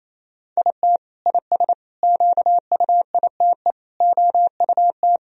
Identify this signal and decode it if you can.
Morse code, heard as a keyed tone, about 27 wpm, 700 Hz. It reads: IT IS QUITE OUT
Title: Sentences from Top 1000 Words - Review Part 2 - 28wpm